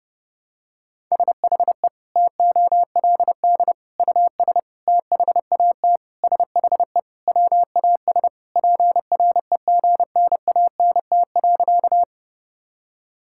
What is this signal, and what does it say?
Morse code, heard as a keyed tone, about 30 wpm, 700 Hz. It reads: SHE TOLD US THAT SHE WAS PREGNANT.
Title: Sentences from Top 1000 Words - Review Part 6 - 30wpm